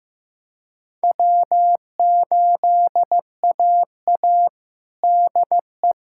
Morse code, heard as a keyed tone, about 15 wpm, 700 Hz. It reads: W8AA DE
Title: USA Callsign DE Callsign - 4 Characters - 15wpm